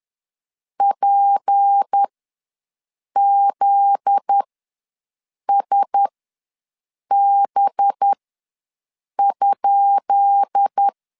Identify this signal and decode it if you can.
Morse code, heard as a keyed tone, about 10 wpm, spaced as for 6 wpm, 800 Hz. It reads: PZSB?
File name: LEZIONI / 10 SIGNS / 10D COMA ws 10wpm 4314